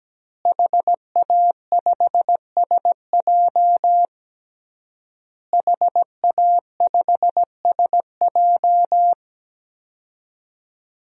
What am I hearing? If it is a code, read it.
Morse code, heard as a keyed tone, about 17 wpm, 700 Hz. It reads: HA5SJ HA5SJ
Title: DXCC-100 Call Signs - 5 Characters - Repeated 3x - 17wpm